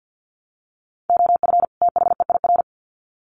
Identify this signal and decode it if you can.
Morse code, heard as a keyed tone, about 50 wpm, 700 Hz. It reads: OF THEIR
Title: Sets of 2 Words 50wpm